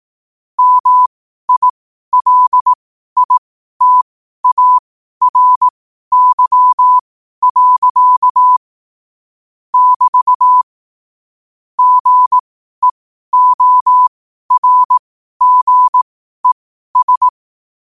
Morse code, heard as a keyed tone, about 18 wpm, spaced as for 12 wpm, 1000 Hz. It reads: MILITARY. <BT> GEORGES
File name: 12wpm